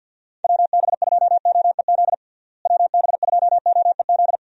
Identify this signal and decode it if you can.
Morse code, heard as a keyed tone, about 52 wpm, 700 Hz. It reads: WB2OEZ WB2OEZ
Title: USA Call Signs - 6 Characters - Repeated 2x - 50wpm